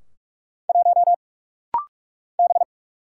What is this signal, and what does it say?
Morse code, heard as a keyed tone, about 44 wpm, 700 Hz. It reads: 1 B
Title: Single Letter-Number - Rapid-Fire 45wpm